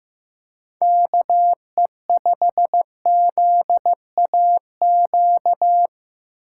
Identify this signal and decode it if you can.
Morse code, heard as a keyed tone, about 15 wpm, 700 Hz. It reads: KE5ZAQ
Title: USA Call Signs - 6 Characters - 3x Character Spacing - 15wpm